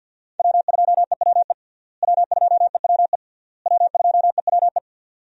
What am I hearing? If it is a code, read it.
Morse code, heard as a keyed tone, about 49 wpm, 700 Hz. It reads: W2EWE W2EWE W2EWE
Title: USA Call Signs - 5 Characters - Repeated 3x - 50wpm